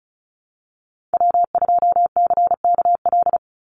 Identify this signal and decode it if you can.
Morse code, heard as a keyed tone, about 34 wpm, 700 Hz. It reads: W2CKL
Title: NA SOTA Activator Callsigns 35wpm